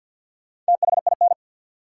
Morse code, heard as a keyed tone, about 47 wpm, 700 Hz. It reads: THIN
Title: Four Letter Words - 50wpm